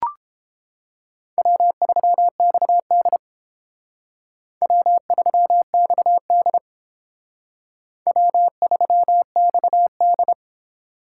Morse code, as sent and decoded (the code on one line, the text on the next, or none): .-- ...-- -..- -.. / .-- ...-- -..- -.. / .-- ...-- -..- -..
W3XD W3XD W3XD